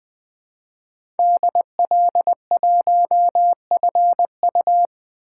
Morse code, heard as a keyed tone, about 20 wpm, 700 Hz. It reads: DL1FU
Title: 2023 SOTA Chaser Call Signs - 20wpm